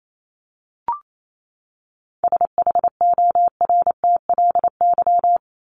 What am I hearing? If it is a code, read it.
Morse code, heard as a keyed tone, about 28 wpm, 700 Hz. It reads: SHORTLY
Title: LY Suffix - Top 100 Words with Base Words 28wpm